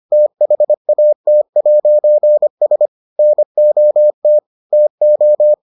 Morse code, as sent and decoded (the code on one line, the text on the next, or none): - .... .- - .----. ... / -. --- - / - ---
THAT'S NOT TO